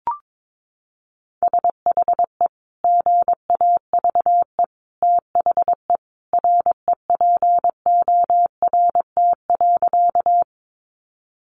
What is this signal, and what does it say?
Morse code, heard as a keyed tone, about 22 wpm, 700 Hz. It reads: SHE GAVE THE REPORT.
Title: Sentences from Top 900 Words 22wpm